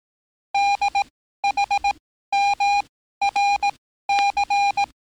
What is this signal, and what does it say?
Morse code, heard as a keyed tone, about 18 wpm, spaced as for 13 wpm, 800 Hz. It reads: DHMRC
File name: training / NAVPERS 11049to11060 / NAVPERS 11054 RA RB